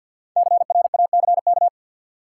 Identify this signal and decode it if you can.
Morse code, heard as a keyed tone, about 53 wpm, 700 Hz. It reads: CRACK
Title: Five Letter Words - 50wpm